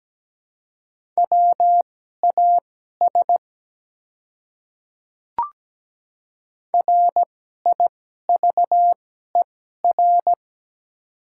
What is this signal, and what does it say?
Morse code, heard as a keyed tone, about 17 wpm, spaced as for 12 wpm, 700 Hz. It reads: WAS RIVER
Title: 2x Character Spacing - Top 300 Words - 17wpm